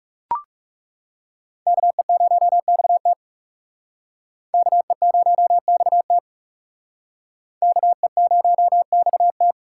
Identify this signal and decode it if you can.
Morse code, heard as a keyed tone, about 39 wpm, 700 Hz. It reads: KE0XT KE0XT KE0XT